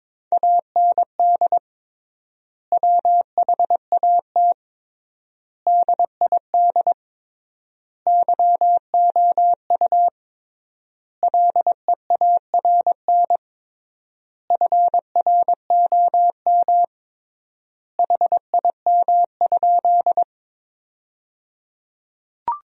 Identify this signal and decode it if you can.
Morse code, heard as a keyed tone, about 22 wpm, 700 Hz. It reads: AND WHAT DID YOU LEARN FROM HIM?
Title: 3x Word-Spacing - Sentences from Top 300 Words - 22wpm